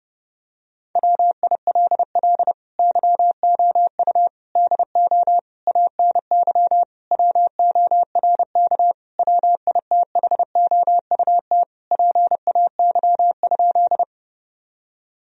Morse code, heard as a keyed tone, about 30 wpm, 700 Hz. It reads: WILL YOU DO ANY WORK WITHOUT PAY?
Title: Sentences from Top 1000 Words - Review Part 3 - 30wpm